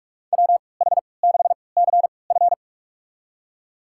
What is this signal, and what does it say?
Morse code, heard as a keyed tone, about 44 wpm, spaced as for 25 wpm, 700 Hz. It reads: WH6CF
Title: USA Call Signs - 3x Character Spacing - 45wpm